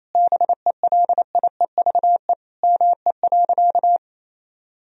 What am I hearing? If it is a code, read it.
Morse code, heard as a keyed tone, about 28 wpm, 700 Hz. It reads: BELIEVE ME.
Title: Sentences from Top 1000 Words - Review Part 3 - 28wpm